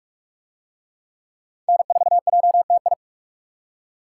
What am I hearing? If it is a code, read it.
Morse code, heard as a keyed tone, about 44 wpm, 700 Hz. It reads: N4JTI